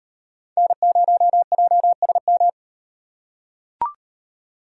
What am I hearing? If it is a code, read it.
Morse code, heard as a keyed tone, about 38 wpm, 700 Hz. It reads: N0JSM